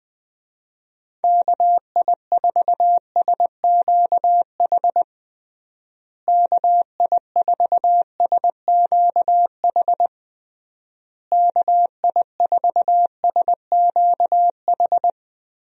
Morse code, heard as a keyed tone, about 20 wpm, 700 Hz. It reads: KI4SQH KI4SQH KI4SQH
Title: USA Call Signs - 6 Characters - Repeated 3x - 20wpm